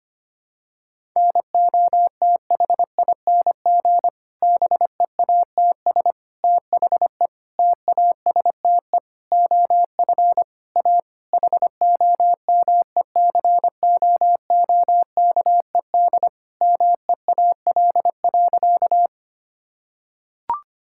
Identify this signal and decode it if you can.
Morse code, heard as a keyed tone, about 25 wpm, 700 Hz. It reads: NOTHING BEATS THE TASTE OF A HOMECOOKED MEAL.
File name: practice / 25wpm / Difficult Sentences with Hints - 25wpm